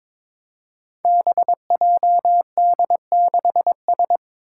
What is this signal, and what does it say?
Morse code, heard as a keyed tone, about 22 wpm, 700 Hz. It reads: BJD6S
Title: Five Challenging Letter-Number Combinations 22wpm